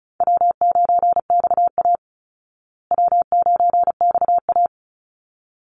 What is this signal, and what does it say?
Morse code, heard as a keyed tone, about 34 wpm, 700 Hz. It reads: W9XA W9XA
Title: Call Signs Repeated 3x - 35wpm